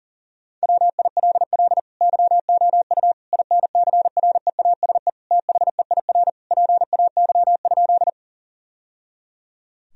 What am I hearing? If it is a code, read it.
Morse code, heard as a keyed tone, about 40 wpm, 700 Hz. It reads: WILL YOU INCREASE THEIR PAY?
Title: Sentences from Top 1000 Words - Review Part 4 - 40wpm